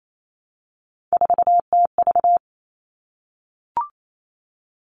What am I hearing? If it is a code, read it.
Morse code, heard as a keyed tone, about 28 wpm, 700 Hz. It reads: 4TV
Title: Three Challenging Letter-Number Combinations 28wpm